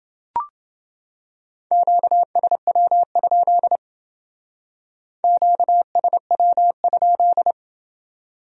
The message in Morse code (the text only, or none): QSW? QSW?